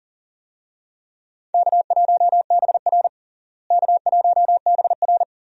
Morse code, heard as a keyed tone, about 40 wpm, 700 Hz. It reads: K1BR K1BR
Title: USA Call Signs - 4 Characters - Repeated 3x - 40wpm